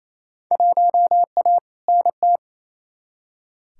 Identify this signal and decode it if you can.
Morse code, heard as a keyed tone, about 28 wpm, 700 Hz. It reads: 1A NT
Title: Field Day Exchanges Encore - 28wpm